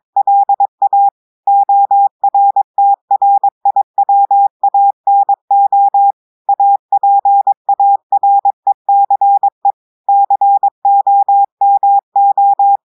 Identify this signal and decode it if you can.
Morse code, heard as a keyed tone, about 22 wpm, 800 Hz. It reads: LA ORTRIWANO APARECE COMO
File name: audios / portugais / portugais texte en clair 1 vitesse 22 mots minutes